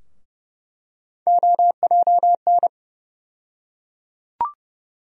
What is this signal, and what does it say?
Morse code, heard as a keyed tone, about 29 wpm, 700 Hz. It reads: OJN
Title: USA Call Sign Postfixes - 3 Letters 30wpm